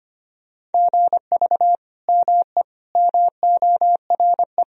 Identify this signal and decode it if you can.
Morse code, heard as a keyed tone, about 25 wpm, 700 Hz. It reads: GV ME MORE
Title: Sentences from Top 200 Words and Abbreviations 25wpm